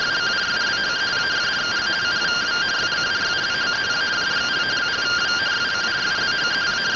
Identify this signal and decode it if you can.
Two-tone teleprinter signal, about 170 Hz shift.
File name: findings / MR CORONAL fr1 -131.gen-14 rtty rx